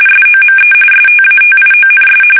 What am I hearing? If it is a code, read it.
RTTY, about 45.45 baud. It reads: SOME STATIONS
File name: fsk850 / teletype-RTTY